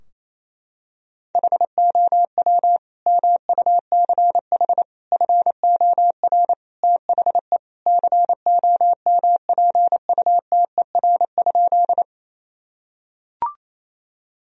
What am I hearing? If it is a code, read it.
Morse code, heard as a keyed tone, about 28 wpm, 700 Hz. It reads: HOW MUCH FOR THE COMPUTER?
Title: Sentences from Top 1000 Words - Review Part 3 - 28wpm